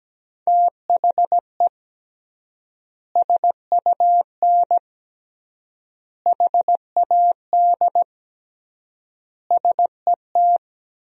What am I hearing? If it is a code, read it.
Morse code, heard as a keyed tone, about 17 wpm, 700 Hz. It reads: THE SUN HAD SET
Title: 3x Word-Spacing - Sentences from Top 300 Words - 17wpm